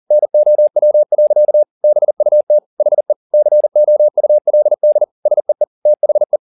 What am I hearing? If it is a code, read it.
Morse code, heard as a keyed tone, about 40 wpm, 600 Hz. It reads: NOW. BUT HE COULD SEE THE